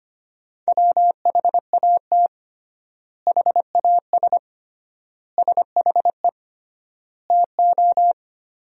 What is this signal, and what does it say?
Morse code, heard as a keyed tone, about 25 wpm, 700 Hz. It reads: WHAT HAS SHE TO
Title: 3x Word-Spacing - Sentences from Top 100 Words - 25wpm